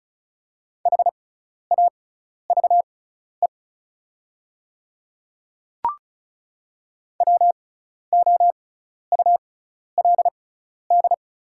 Morse code, heard as a keyed tone, about 34 wpm, spaced as for 12 wpm, 700 Hz. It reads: HAVE WOULD
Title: 6x Character Spacing - Top 200 Words - 35wpm